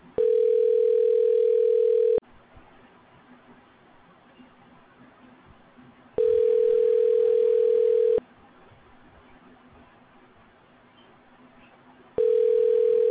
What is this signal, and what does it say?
Ringback tone (North America), 440 + 480 Hz.